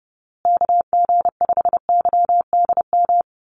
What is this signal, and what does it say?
Morse code, heard as a keyed tone, about 30 wpm, 700 Hz. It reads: KG5YDM
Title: USA Call Signs Repeated 3x 30wpm